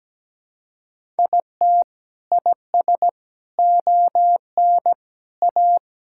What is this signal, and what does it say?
Morse code, heard as a keyed tone, about 17 wpm, 700 Hz. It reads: IT IS ON A
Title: Sentences from Top 1000 Words - Review Part 5 - 17wpm